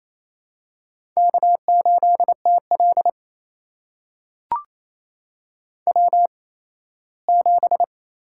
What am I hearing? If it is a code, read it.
Morse code, heard as a keyed tone, about 28 wpm, spaced as for 17 wpm, 700 Hz. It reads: K8TL W7
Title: USA Call Signs - 4 Characters - 8x Character Spacing - 28wpm